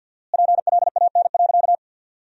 Morse code, heard as a keyed tone, about 50 wpm, 700 Hz. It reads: PLAN.
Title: 4x Word-Spacing - Sentences from Top 500 Words - 50wpm